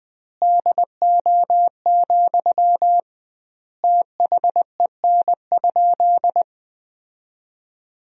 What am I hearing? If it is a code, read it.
Morse code, heard as a keyed tone, about 20 wpm, 700 Hz. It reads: DO, THEN?
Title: 2x Word-Spacing - Sentences from Top 100 Words - 20wpm